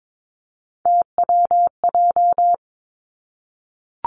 Morse code, heard as a keyed tone, about 22 wpm, 700 Hz. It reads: TWJ
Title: Three Challenging Letter-Number Combinations 22wpm